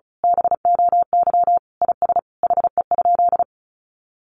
Morse code, heard as a keyed tone, about 34 wpm, 700 Hz. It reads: BOY IS HE?
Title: Sentences from Top 1300 Words 35wpm